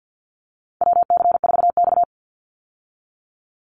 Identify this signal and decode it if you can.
Morse code, heard as a keyed tone, about 50 wpm, 700 Hz. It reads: WC4X
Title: Call Signs Repeated 3x - 50wpm